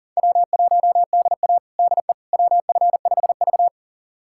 Morse code, heard as a keyed tone, about 40 wpm, 700 Hz. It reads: W1DA DE WF5V
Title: USA Callsign DE Callsign - 4 Characters - 40wpm